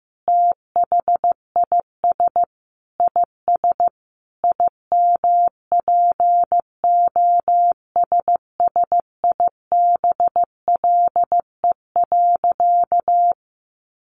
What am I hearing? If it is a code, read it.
Morse code, heard as a keyed tone, about 15 wpm, 700 Hz. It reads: THIS IS IMPOSSIBLE.
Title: Sentences from Top 1500 Words 15wpm